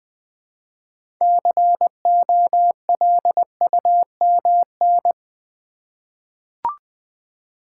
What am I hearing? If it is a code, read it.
Morse code, heard as a keyed tone, about 20 wpm, 700 Hz. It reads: COLUMN
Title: 6 Letter Words - First 3 Letters Given - 20wpm